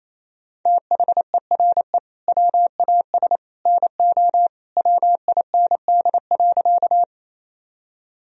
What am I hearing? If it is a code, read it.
Morse code, heard as a keyed tone, about 27 wpm, 700 Hz. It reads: THERE WAS NO WIND.
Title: Sentences from Top 1000 Words - Review Part 2 - 28wpm